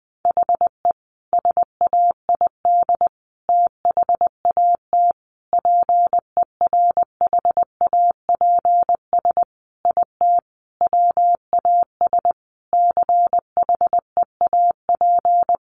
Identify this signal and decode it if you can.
Morse code, heard as a keyed tone, about 19 wpm, 700 Hz. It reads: HE SAID THAT PERHAPS IT WAS CHEAP
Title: Sentences from Top 1200 Words 20wpm